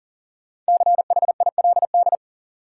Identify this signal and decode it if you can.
Morse code, heard as a keyed tone, about 39 wpm, 700 Hz. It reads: CHILD
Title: Sets of 5 Words Encore - New Word Drill - 40wpm